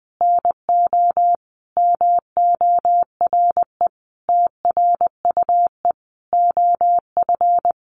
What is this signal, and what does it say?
Morse code, heard as a keyed tone, about 19 wpm, 700 Hz. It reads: NO MORE TRUE OF
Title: Sentences from Top 1400 Words 20wpm